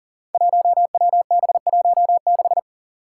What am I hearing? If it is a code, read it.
Morse code, heard as a keyed tone, about 40 wpm, 700 Hz. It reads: 1WB16
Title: Five Challenging Letter-Number Combinations 40wpm